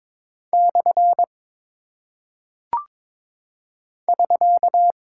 Morse code, heard as a keyed tone, about 22 wpm, 700 Hz. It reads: / <SK>
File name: practice / 22wpm / Single Punctuation-Prosign - 22wpm